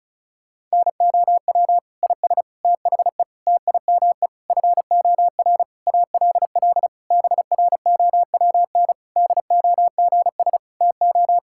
Morse code, heard as a keyed tone, about 36 wpm, 700 Hz. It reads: NOW IS THE TIME FOR ALL BROWN DOGS TO
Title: Pangram Challenge v2 - 35wpm